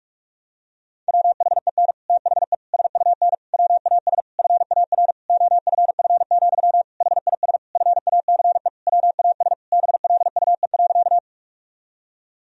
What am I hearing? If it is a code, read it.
Morse code, heard as a keyed tone, about 46 wpm, 700 Hz. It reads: WHEN THE SUN WAS FAR OFF, HIS FACE WAS BLUE.
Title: Sentences from Top 1000 Words - Review Part 3 - 45wpm